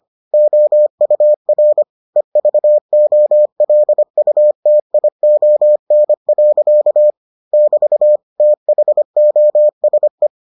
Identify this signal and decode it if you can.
Morse code, heard as a keyed tone, about 25 wpm, 600 Hz. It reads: OUR EVOLUTION. <BT> THOSE